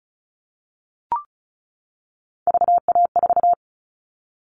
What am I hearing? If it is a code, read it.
Morse code, heard as a keyed tone, about 34 wpm, 700 Hz. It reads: VA4
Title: All Cut Numbers - 3-Digit 35wpm